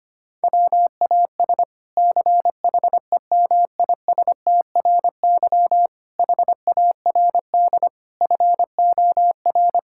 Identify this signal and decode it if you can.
Morse code, heard as a keyed tone, about 25 wpm, 700 Hz. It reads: WAS CHEMISTRY HARD FOR
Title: Sentences from Top 1000 Words - Review Part 6 - 25wpm